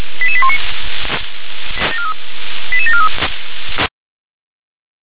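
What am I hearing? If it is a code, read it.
Five-tone selective call (ZVEI), heard as quick runs of five tones, tones of about 70 ms. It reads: 88819 88743
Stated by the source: Five-tone ZVEI selective call received with an SDR.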